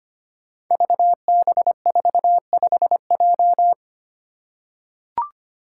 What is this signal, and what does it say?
Morse code, heard as a keyed tone, about 25 wpm, 700 Hz. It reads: VB45J
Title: Code Groups - 5 Letter:Numbers - 25wpm